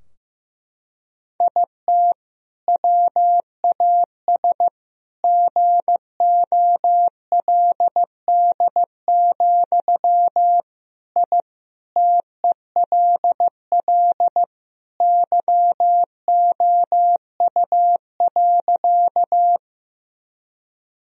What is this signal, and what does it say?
Morse code, heard as a keyed tone, about 15 wpm, 700 Hz. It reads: IT WAS GOLD, I TELL YOU.
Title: Sentences from Top 1000 Words - Review Part 3 - 15wpm